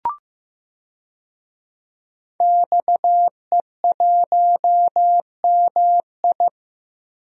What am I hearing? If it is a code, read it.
Morse code, heard as a keyed tone, about 15 wpm, 700 Hz. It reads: XE1MI
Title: DXCC-100 Call Signs - 5 Characters - Repeated 2x - 15wpm